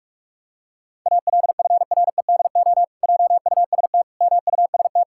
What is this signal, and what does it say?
Morse code, heard as a keyed tone, about 46 wpm, 700 Hz. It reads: ALFREDO JUST MUST
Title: Pangram Challenge v2 - 45wpm